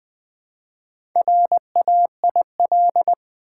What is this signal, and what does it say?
Morse code, heard as a keyed tone, about 20 wpm, 700 Hz. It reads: RAIL
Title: Build A Word - 1 Letter At A Time - 4-Letter Words - 20wpm